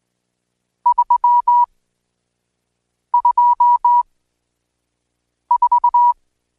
Morse code, heard as a keyed tone, about 19 wpm, spaced as for 5 wpm, 970 Hz. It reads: 324
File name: Code Quick